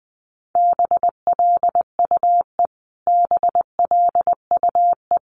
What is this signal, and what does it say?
Morse code, heard as a keyed tone, about 20 wpm, 700 Hz. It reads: BLUE BLUE
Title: Top 500 Words Repeated 20wpm